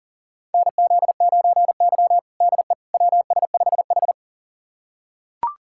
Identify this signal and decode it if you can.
Morse code, heard as a keyed tone, about 40 wpm, 700 Hz. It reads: NZ9Y DE WS5H
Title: USA Callsign DE Callsign - 4 Characters - 40wpm